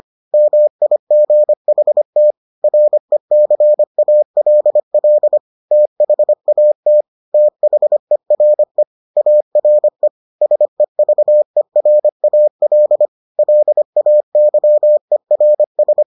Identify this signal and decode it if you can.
Morse code, heard as a keyed tone, about 25 wpm, 600 Hz. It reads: MIGHT RECALL THAT THERE ARE SEVERAL LAYERS